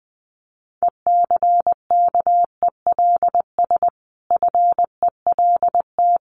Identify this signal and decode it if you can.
Morse code, heard as a keyed tone, about 20 wpm, 700 Hz. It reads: ECKELS FELT
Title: A Sound Of Thunder 20wpm